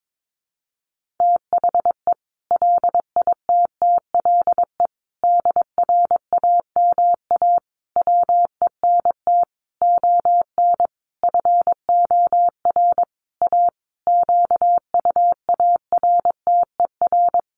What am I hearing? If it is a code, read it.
Morse code, heard as a keyed tone, about 22 wpm, 700 Hz. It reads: THE LITTLE DRAMA WENT ON FOR A QUARTER
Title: Sentences from Top 1500 Words 22wpm